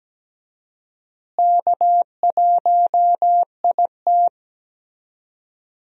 Morse code, heard as a keyed tone, about 17 wpm, 700 Hz. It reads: K1IT
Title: USA Call Signs - 4 Characters - Repeated 2x - 17wpm